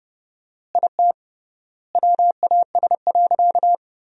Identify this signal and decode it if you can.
Morse code, heard as a keyed tone, about 30 wpm, 700 Hz. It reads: IT WAS.
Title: 3x Word-Spacing - Sentences from Top 100 Words - 30wpm